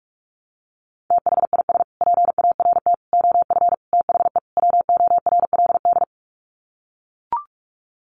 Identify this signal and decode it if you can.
Morse code, heard as a keyed tone, about 44 wpm, 700 Hz. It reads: THIS PART OF THE WORLD
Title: Sets of 5 Words Encore2 45wpm